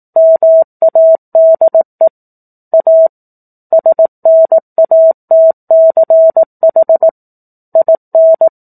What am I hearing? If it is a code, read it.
Morse code, heard as a keyed tone, about 18 wpm, 650 Hz. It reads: MADE A SNATCH IN
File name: 04-Chapter-04 18 WPM